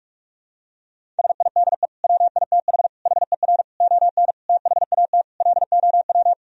Morse code, heard as a keyed tone, about 46 wpm, 700 Hz. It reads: SIDE WITH HER ON THAT LOW
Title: Sentences from Top 1000 Words - Review Part 1 - 45wpm